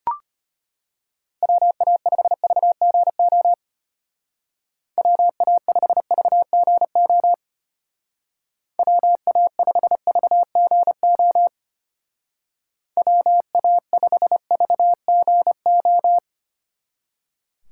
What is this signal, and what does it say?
Morse code, heard as a keyed tone, about 31 wpm, 700 Hz. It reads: WA5VGO WA5VGO WA5VGO WA5VGO
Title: Speed Racing - USA 2x3 Call Signs 25wpm